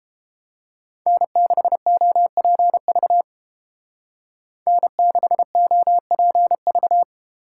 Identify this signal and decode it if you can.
Morse code, heard as a keyed tone, about 29 wpm, 700 Hz. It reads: N6OPV N6OPV